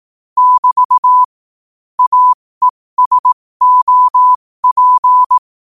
Morse code, heard as a keyed tone, about 18 wpm, spaced as for 15 wpm, 1000 Hz. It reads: <BT> AESOP
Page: qotd-at-15-wpm